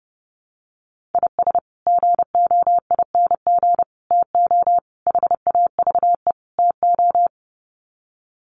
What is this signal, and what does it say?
Morse code, heard as a keyed tone, about 30 wpm, 700 Hz. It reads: IS GOING TO HAVE TO
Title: Sets of 5 Words 30wpm